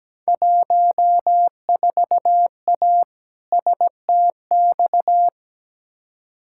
Morse code, heard as a keyed tone, about 17 wpm, 700 Hz. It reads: 14A STX
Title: Field Day Exchanges Encore - 17wpm